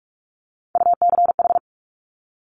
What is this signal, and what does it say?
Morse code, heard as a keyed tone, about 46 wpm, 700 Hz. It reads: UCH
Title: Top Three Letter Combinations in Words 45wpm